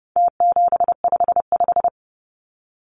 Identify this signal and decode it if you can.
Morse code, heard as a keyed tone, about 30 wpm, 700 Hz. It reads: T755
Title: Numbers - Contest Style - 4-Digit 30wpm